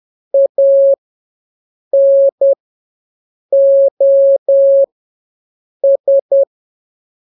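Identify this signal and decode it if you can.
Morse code, heard as a keyed tone, about 10 wpm, spaced as for 6 wpm, 550 Hz. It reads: ANOS
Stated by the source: Telegraphy radio challenge for scouts in the Jamboree-On-The-Air.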